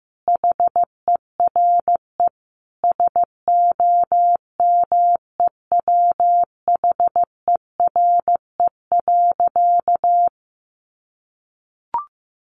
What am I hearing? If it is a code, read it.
Morse code, heard as a keyed tone, about 15 wpm, 700 Hz. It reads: HERE SOMEWHERE.
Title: Sentences from Top 1400 Words 15wpm